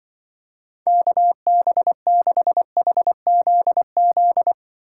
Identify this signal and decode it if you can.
Morse code, heard as a keyed tone, about 25 wpm, 700 Hz. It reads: KB6HZZ
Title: Speed Racing - USA 2x3 Call Signs 20wpm